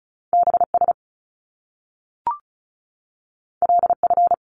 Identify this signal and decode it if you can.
Morse code, heard as a keyed tone, about 36 wpm, 700 Hz. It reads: BS LF
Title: Top Two Letter Combinations in Words 35wpm